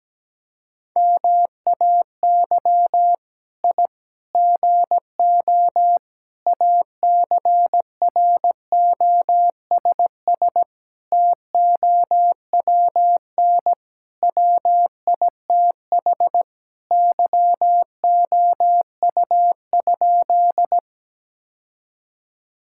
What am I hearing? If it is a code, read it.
Morse code, heard as a keyed tone, about 17 wpm, 700 Hz. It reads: MAY I GO ACROSS TOWN WITH YOU?
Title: Sentences from Top 1000 Words - Review Part 5 - 17wpm